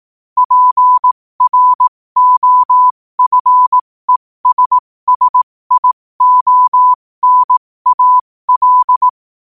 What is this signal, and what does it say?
Morse code, heard as a keyed tone, about 18 wpm, spaced as for 15 wpm, 1000 Hz. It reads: PROFESSIONAL
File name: qotdfiles / 15wpm